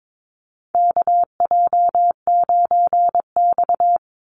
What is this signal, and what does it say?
Morse code, heard as a keyed tone, about 22 wpm, 700 Hz. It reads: KJ9X
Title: NA SOTA Chaser Callsigns 22wpm